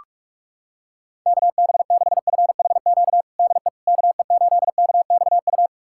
Morse code, heard as a keyed tone, about 46 wpm, 700 Hz. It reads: KB6FHQ DE KE8KXU
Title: USA Callsign DE Callsign 45wpm